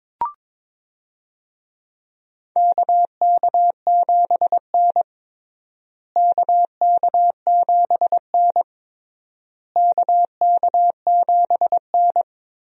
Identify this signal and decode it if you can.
Morse code, heard as a keyed tone, about 22 wpm, 700 Hz. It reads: KK7N KK7N KK7N
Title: USA Call Signs - 4 Characters - Repeated 3x - 22wpm